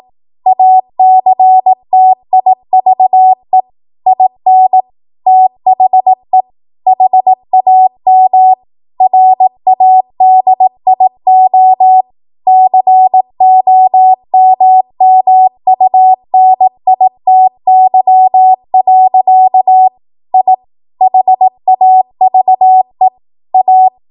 Morse code, heard as a keyed tone, about 18 wpm, 750 Hz. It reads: ACTIVE IN THE HAM RADIO COMMUNITY. I HAVE A
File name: Archive / 18 WPM / 18WPM